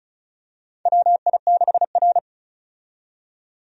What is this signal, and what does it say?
Morse code, heard as a keyed tone, about 34 wpm, 700 Hz. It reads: WI6R
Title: USA Call Signs - 4 Characters - 35wpm